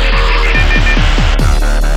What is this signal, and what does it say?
There is no code.